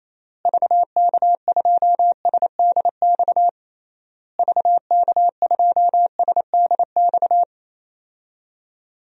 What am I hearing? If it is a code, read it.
Morse code, heard as a keyed tone, about 28 wpm, 700 Hz. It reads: VK2SDX VK2SDX
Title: DXCC-100 Call Signs - 6 Characters - Repeated 3x - 28wpm